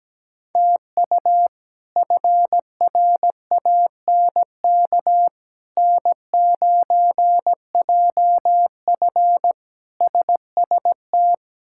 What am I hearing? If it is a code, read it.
Morse code, heard as a keyed tone, about 17 wpm, 700 Hz. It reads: TU FRANK N9JF SST
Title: K1USN SST QSO Practice - 17wpm